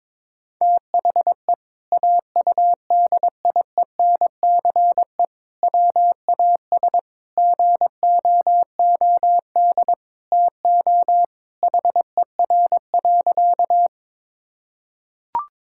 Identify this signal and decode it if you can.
Morse code, heard as a keyed tone, about 22 wpm, 700 Hz. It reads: THE AUDIENCE WAS GOOD TO HER.
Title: Sentences from Top 1000 Words - Review Part 5 - 22wpm